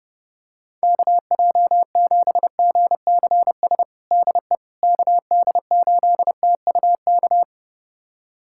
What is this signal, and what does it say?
Morse code, heard as a keyed tone, about 32 wpm, 700 Hz. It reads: KJ7GCS DE KD8TUK
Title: USA Callsign DE Callsign 30wpm